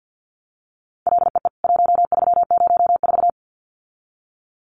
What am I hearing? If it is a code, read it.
Morse code, heard as a keyed tone, about 50 wpm, 700 Hz. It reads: LEE 1304
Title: CWT Exchange 50wpm